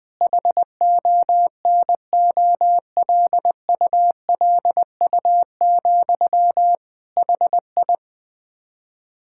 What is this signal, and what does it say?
Morse code, heard as a keyed tone, about 20 wpm, 700 Hz. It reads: HONOLULU, HI
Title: Practice QSO - Exchange 1 - Beginning - 20wpm